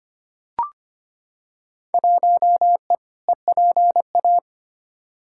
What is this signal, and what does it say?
Morse code, heard as a keyed tone, about 25 wpm, 700 Hz. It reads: 1E EPA
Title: Field Day Exchanges - 25wpm